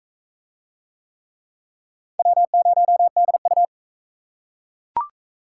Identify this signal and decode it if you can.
Morse code, heard as a keyed tone, about 37 wpm, 700 Hz. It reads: W0DU